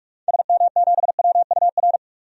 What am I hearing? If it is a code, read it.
Morse code, heard as a keyed tone, about 42 wpm, 700 Hz. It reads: SM7WUR
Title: DXCC-100 Call Signs - 6 Characters - 45wpm